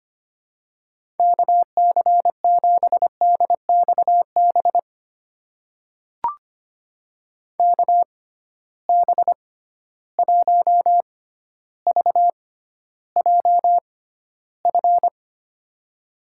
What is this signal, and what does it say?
Morse code, heard as a keyed tone, about 25 wpm, spaced as for 13 wpm, 700 Hz. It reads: KC7DXB KB1VJF
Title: USA Call Signs - 6 Characters - 6x Character Spacing - 25wpm